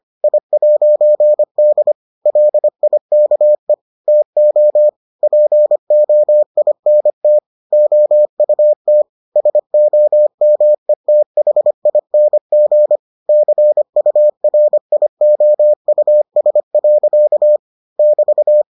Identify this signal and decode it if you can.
Morse code, heard as a keyed tone, about 25 wpm, 600 Hz. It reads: I'D LIKE TO POINT OUT SOMETHING CURIOUS. <BT>